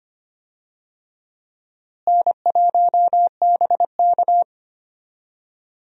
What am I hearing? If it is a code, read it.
Morse code, heard as a keyed tone, about 25 wpm, 700 Hz. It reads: N1BK
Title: USA Call Signs - 4 Characters - Repeated 2x - 25wpm